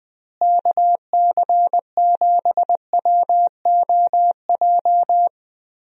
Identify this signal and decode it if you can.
Morse code, heard as a keyed tone, about 20 wpm, 700 Hz. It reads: KC7WOJ
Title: USA Call Signs - 6 Characters - Repeated 3x - 20wpm